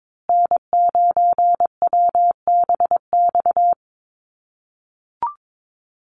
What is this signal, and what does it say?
Morse code, heard as a keyed tone, about 22 wpm, 700 Hz. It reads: N9WBX
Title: USA Call Signs Repeated 2x 22wpm